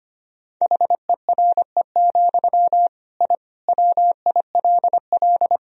Morse code, heard as a keyed tone, about 25 wpm, 700 Hz. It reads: HERE, I WILL
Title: Sentences from Top 1000 Words - Review Part 1 - 25wpm